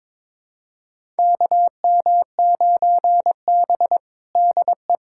Morse code, heard as a keyed tone, about 22 wpm, 700 Hz. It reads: KM9B DE
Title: USA Callsign DE Callsign - 4 Characters - 22wpm